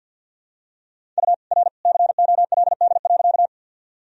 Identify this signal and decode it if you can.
Morse code, heard as a keyed tone, about 52 wpm, 700 Hz. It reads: U R COLD.
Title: Sentences from Top 400 Words and Abbreviations 50wpm